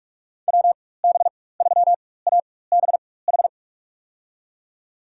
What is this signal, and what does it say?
Morse code, heard as a keyed tone, about 46 wpm, spaced as for 21 wpm, 690 Hz. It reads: WB3ABH
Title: USA Call Signs - 4x Character Spacing - 45wpm